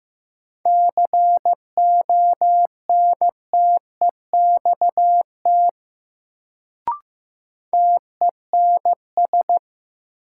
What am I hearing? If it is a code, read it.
Morse code, heard as a keyed tone, about 15 wpm, 700 Hz. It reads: CONTEXT TENS
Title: Sets of 4 Words - New Word Drill - 15wpm